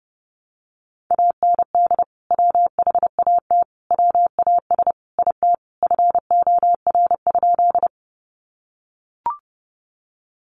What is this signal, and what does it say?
Morse code, heard as a keyed tone, about 30 wpm, 700 Hz. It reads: AND WHAT WAS IT FOR?
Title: Sentences from Top 100 Words 30wpm